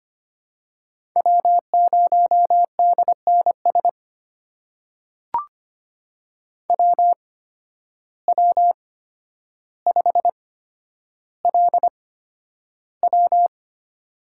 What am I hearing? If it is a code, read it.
Morse code, heard as a keyed tone, about 25 wpm, spaced as for 11 wpm, 700 Hz. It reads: W0DNS WW5LW
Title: USA Call Signs - 5 Characters - 8x Character Spacing - 25wpm